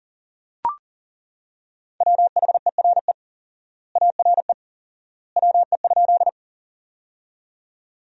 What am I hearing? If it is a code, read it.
Morse code, heard as a keyed tone, about 40 wpm, 700 Hz. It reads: WHERE ARE WE?
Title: 4x Word-Spacing - Sentences from Top 200 Words - 40wpm